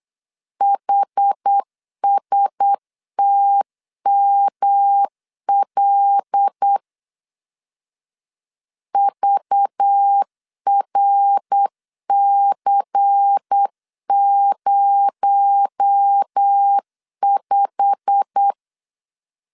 Morse code, heard as a keyed tone, about 8 wpm, 800 Hz. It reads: HSTML VRC05
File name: LEZIONI / 8 YJ45 / 8 YJ45 8wpm 3214